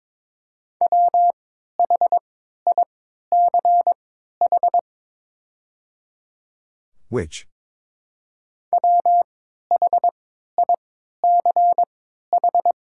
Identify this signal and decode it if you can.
Morse code, heard as a keyed tone, about 22 wpm, spaced as for 12 wpm, 700 Hz. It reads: WHICH WHICH
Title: 3x Character Spacing - Top 100 Words - 22wpm